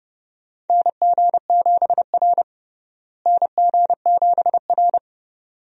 Morse code, heard as a keyed tone, about 32 wpm, 700 Hz. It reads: NG7R NG7R
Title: USA Call Signs - 4 Characters - Repeated 2x - 30wpm